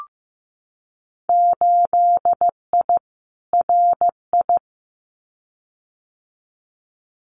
Morse code, heard as a keyed tone, about 16 wpm, 700 Hz. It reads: 8I RI
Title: Winter Field Day Exchanges Encore 15wpm